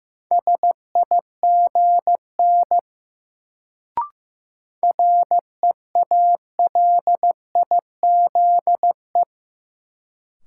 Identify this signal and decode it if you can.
Morse code, heard as a keyed tone, about 15 wpm, 700 Hz. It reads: SIGN REALIZE
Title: Sentences from Top 1000 Words - New Word Drill - 15wpm